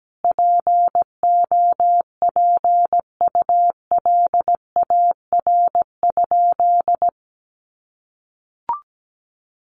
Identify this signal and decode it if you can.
Morse code, heard as a keyed tone, about 17 wpm, 700 Hz. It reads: POPULAR?
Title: Sentences from Top 1200 Words 17wpm